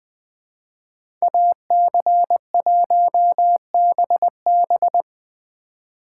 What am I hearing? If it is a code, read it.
Morse code, heard as a keyed tone, about 20 wpm, 700 Hz. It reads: AC1BB
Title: Speed Racing - USA 2x2 Call Signs 17wpm